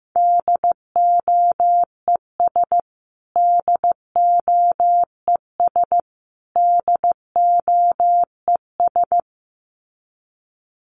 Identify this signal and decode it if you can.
Morse code, heard as a keyed tone, about 15 wpm, 700 Hz. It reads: DOES DOES DOES
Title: Top 500 Words Repeated 15wpm